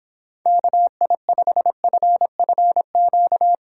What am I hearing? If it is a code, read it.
Morse code, heard as a keyed tone, about 26 wpm, 700 Hz. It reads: KI5FFQ